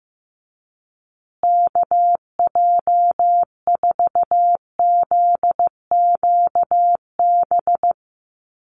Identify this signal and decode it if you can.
Morse code, heard as a keyed tone, about 15 wpm, 700 Hz. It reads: KJ4ZQB
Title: USA Call Signs Repeated 2x 15wpm